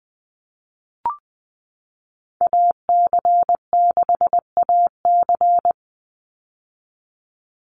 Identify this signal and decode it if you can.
Morse code, heard as a keyed tone, about 20 wpm, 700 Hz. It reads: AC6AC
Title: Call Signs - 20wpm